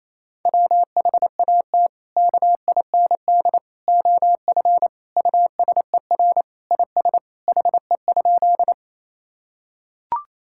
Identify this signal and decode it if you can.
Morse code, heard as a keyed tone, about 28 wpm, 700 Hz. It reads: WHAT KIND OF USER IS HE?
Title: Sentences from Top 1000 Words - Review Part 5 - 28wpm